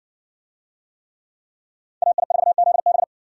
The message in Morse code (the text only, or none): AI4ZB